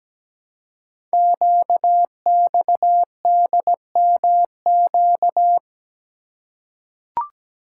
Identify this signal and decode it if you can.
Morse code, heard as a keyed tone, about 17 wpm, 700 Hz. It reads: QXDMQ
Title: Code Groups - 5 Letters - 17wpm